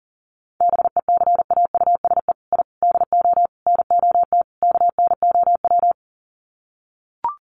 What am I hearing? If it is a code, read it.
Morse code, heard as a keyed tone, about 37 wpm, 700 Hz. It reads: BECAUSE I DO NOT KNOW
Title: Sets of 5 Words Encore2 40wpm